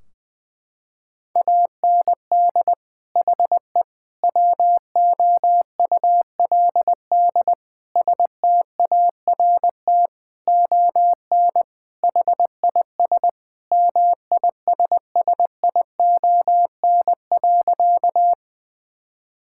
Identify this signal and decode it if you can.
Morse code, heard as a keyed tone, about 20 wpm, 700 Hz. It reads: AND HE WOULD START ON HIS MISSION.
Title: Sentences from Top 1600 Words 20wpm